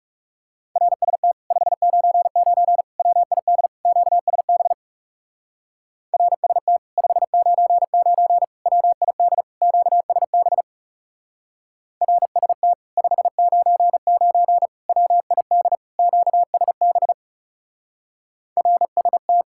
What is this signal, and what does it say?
Morse code, heard as a keyed tone, about 40 wpm, 700 Hz. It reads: RST 599 WID QSB RST 599 WID QSB RST 599 WID QSB RST